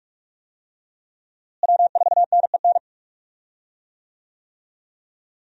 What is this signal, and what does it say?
Morse code, heard as a keyed tone, about 42 wpm, 700 Hz. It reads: W4NEN
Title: USA Call Signs - 5 Characters - Repeated 2x - 45wpm